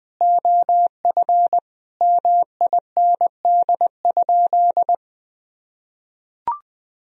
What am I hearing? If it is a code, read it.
Morse code, heard as a keyed tone, about 20 wpm, 700 Hz. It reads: OF MIND?
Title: Sentences from Top 1000 Words - Review Part 2 - 20wpm